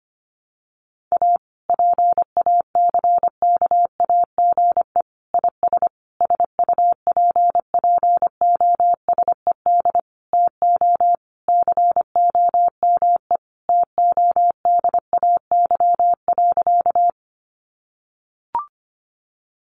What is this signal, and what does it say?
Morse code, heard as a keyed tone, about 25 wpm, 700 Hz. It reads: A PACKAGE IS SUPPOSED TO COME TODAY.
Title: Additional Easy Phrases 25wpm